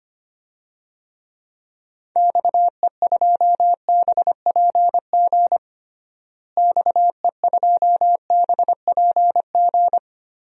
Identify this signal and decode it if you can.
Morse code, heard as a keyed tone, about 25 wpm, 700 Hz. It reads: XE2BPG XE2BPG
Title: DXCC-100 Call Signs - 6 Characters - Repeated 3x - 25wpm